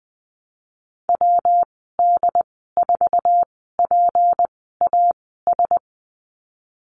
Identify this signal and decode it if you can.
Morse code, heard as a keyed tone, about 20 wpm, spaced as for 15 wpm, 700 Hz. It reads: WD4PAS
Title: 179 Lesson 29g - Slash - Call Signs 20wpm